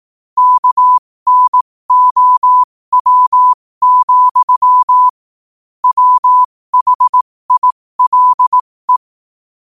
Morse code, heard as a keyed tone, about 18 wpm, spaced as for 15 wpm, 1000 Hz. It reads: KNOW, WHILE
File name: qotdfiles / 15wpm